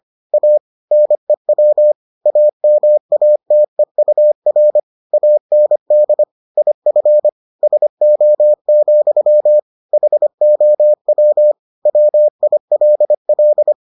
Morse code, heard as a keyed tone, about 25 wpm, 600 Hz. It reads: A NEW AMATEUR AND IF SO, HOW WILL